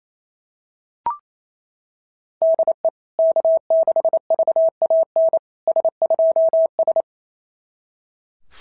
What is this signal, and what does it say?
Morse code, heard as a keyed tone, about 27 wpm, 660 Hz. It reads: DE K6VAN S2S
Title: SOTA QSO Practice 28wpm